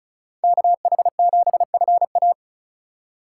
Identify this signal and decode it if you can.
Morse code, heard as a keyed tone, about 34 wpm, 700 Hz. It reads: KH7FA
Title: USA Call Signs - 5 Characters - 4x Character Spacing - 35wpm